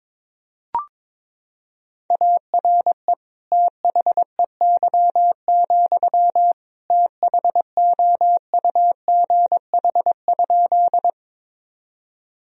Morse code, heard as a keyed tone, about 22 wpm, 700 Hz. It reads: ARE THEY, THOUGH?
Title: Sentences from Top 1000 Words - Review Part 2 - 22wpm